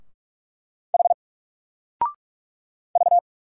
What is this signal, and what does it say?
Morse code, heard as a keyed tone, about 45 wpm, 700 Hz. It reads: H V
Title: Single Letter-Number 45wpm